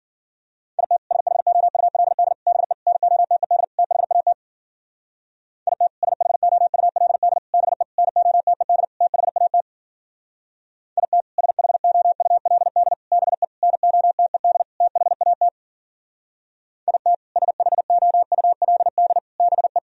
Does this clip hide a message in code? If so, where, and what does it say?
Yes: Morse code from 0.8 to 19.8 s, heard as a keyed tone, about 50 wpm, 700 Hz. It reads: IT SHOULD BE NOTED THAT IT SHOULD BE NOTED THAT IT SHOULD BE NOTED THAT IT SHOULD BE